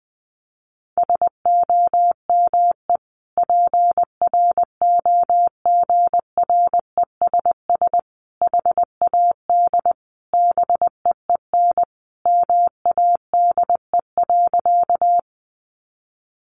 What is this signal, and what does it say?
Morse code, heard as a keyed tone, about 20 wpm, 700 Hz. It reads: SOME PROGRESS HAD BEEN MADE.
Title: Sentences from Top 1400 Words 20wpm